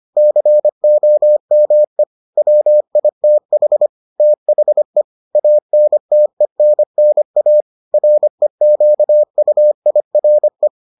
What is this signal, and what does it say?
Morse code, heard as a keyed tone, about 25 wpm, 600 Hz. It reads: COME WITH THE ANTENNA REQUIRE